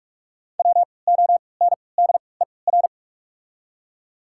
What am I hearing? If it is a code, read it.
Morse code, heard as a keyed tone, about 46 wpm, spaced as for 26 wpm, 690 Hz. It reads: WONDER
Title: 3x Character Spacing - Top 500 Words - 45wpm